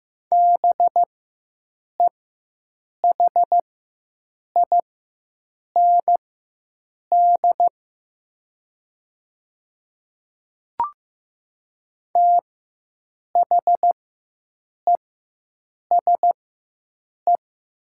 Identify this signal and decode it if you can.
Morse code, heard as a keyed tone, about 15 wpm, spaced as for 7 wpm, 700 Hz. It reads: BEHIND THESE
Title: 4x Character Spacing - Top 500 Words - 15wpm